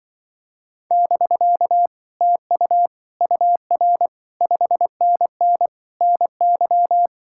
Morse code, heard as a keyed tone, about 24 wpm, 700 Hz. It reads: <BK> TU UR 5NN NY
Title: Speed-Racing Common QSO Phrases - 20wpm